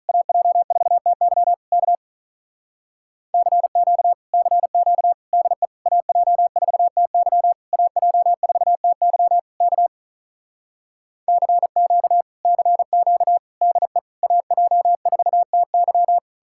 Morse code, heard as a keyed tone, about 48 wpm, 700 Hz. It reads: AJ4TY K CQ CQ DE AJ4TY AJ4TY K CQ CQ DE AJ4TY